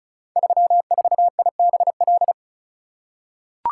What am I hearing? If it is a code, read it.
Morse code, heard as a keyed tone, about 36 wpm, 700 Hz. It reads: 34IBL
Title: Code Groups - 5 Letter:Numbers - 35wpm